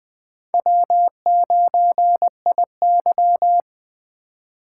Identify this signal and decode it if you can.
Morse code, heard as a keyed tone, about 20 wpm, 700 Hz. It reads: W9IY